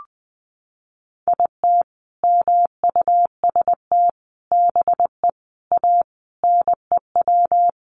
Morse code, heard as a keyed tone, about 20 wpm, 700 Hz. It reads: IT MUST BE A NEW
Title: Sentences from Top 500 Words 20wpm